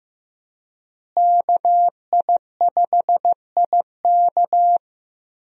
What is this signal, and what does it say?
Morse code, heard as a keyed tone, about 15 wpm, 700 Hz. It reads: KI5IK
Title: USA Call Signs - 5 Characters - Repeated 2x - 15wpm